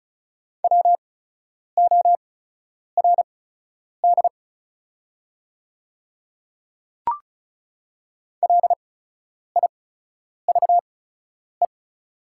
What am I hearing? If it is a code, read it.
Morse code, heard as a keyed tone, about 34 wpm, spaced as for 10 wpm, 690 Hz. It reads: WORD LIVE
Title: 8x Character Spacing - Top 200 Words - 35wpm